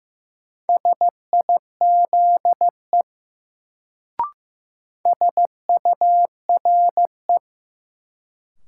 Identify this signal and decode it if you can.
Morse code, heard as a keyed tone, about 15 wpm, 700 Hz. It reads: SIZE SURE
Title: Sets of 3 Words - New Word Drill - 15wpm